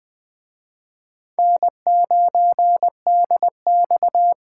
Morse code, heard as a keyed tone, about 19 wpm, 700 Hz. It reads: N9DX